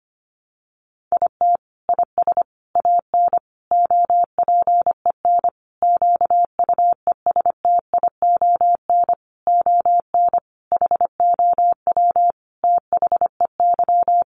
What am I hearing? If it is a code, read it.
Morse code, heard as a keyed tone, about 25 wpm, 700 Hz. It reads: IT IS AN OPEN QUESTION ON HOW THEY
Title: Sentences from Top 1300 Words 25wpm